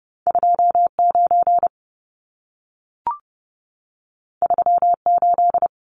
Morse code, heard as a keyed tone, about 30 wpm, 700 Hz. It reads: 29 38
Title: Numbers - 2-Digit 30wpm